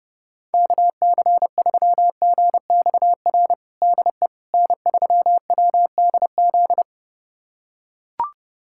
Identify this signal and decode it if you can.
Morse code, heard as a keyed tone, about 32 wpm, 700 Hz. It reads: KC3GXR DE N3WDZ
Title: USA Callsign DE Callsign 30wpm